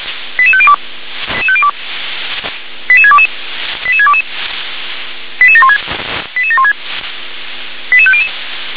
Five-tone selective call (ZVEI), heard as quick runs of five tones, tones of about 70 ms. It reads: 88502 88622 88622 88716 88716 88500